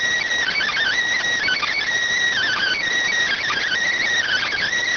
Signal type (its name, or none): Emergency Alert System (SAME) tones